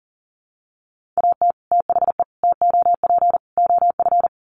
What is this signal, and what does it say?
Morse code, heard as a keyed tone, about 40 wpm, 700 Hz. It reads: AT THE TOP OF
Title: Sets of 4 Words 40wpm